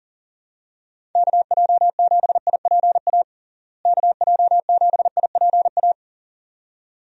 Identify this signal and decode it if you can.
Morse code, heard as a keyed tone, about 40 wpm, 700 Hz. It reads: KJ7IPA KJ7IPA
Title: USA Call Signs - 6 Characters - Repeated 2x - 40wpm